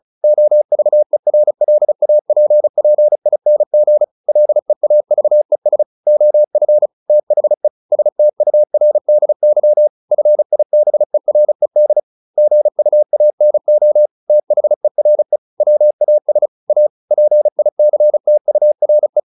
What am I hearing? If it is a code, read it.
Morse code, heard as a keyed tone, about 34 wpm, 600 Hz. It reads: OVERLAPPING LEAVES OF THE STURDY FIBERED GUANO THERE WAS A PICTURE
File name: the old man and the sea / the old man and the sea 1 35wpm